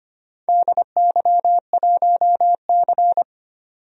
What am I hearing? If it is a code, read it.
Morse code, heard as a keyed tone, about 25 wpm, 700 Hz. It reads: DY1C
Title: DXCC-100 Call Signs - 4 Characters - 25wpm